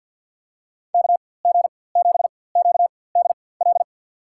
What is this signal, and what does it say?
Morse code, heard as a keyed tone, about 52 wpm, spaced as for 23 wpm, 700 Hz. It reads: KG7QDL
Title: USA Call Signs - 6 Characters - 4x Character Spacing - 50wpm